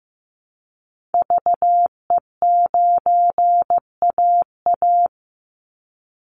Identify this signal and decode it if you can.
Morse code, heard as a keyed tone, about 15 wpm, 700 Hz. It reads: VE9AA
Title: Call Signs Repeated 3x - 15wpm